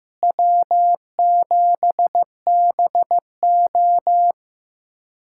Morse code, heard as a keyed tone, about 15 wpm, 700 Hz. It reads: W7BO
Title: USA Callsign DE Callsign - 4 Characters - 15wpm